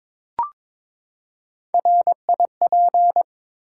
Morse code, heard as a keyed tone, about 23 wpm, 700 Hz. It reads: RIP
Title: Three Letter Words - 22wpm